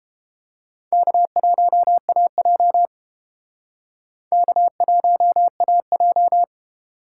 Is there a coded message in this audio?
Yes: Morse code from 0.9 to 6.4 s, heard as a keyed tone, about 31 wpm, 700 Hz. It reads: K1AJ K1AJ